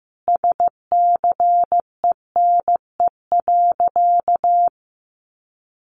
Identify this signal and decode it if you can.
Morse code, heard as a keyed tone, about 15 wpm, 700 Hz. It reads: SCENE.
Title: Sentences from Top 1100 Words 15wpm